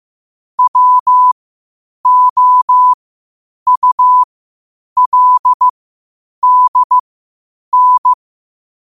Morse code, heard as a keyed tone, about 15 wpm, spaced as for 8 wpm, 1000 Hz. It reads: WOULDN
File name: qotdfiles / 8wpm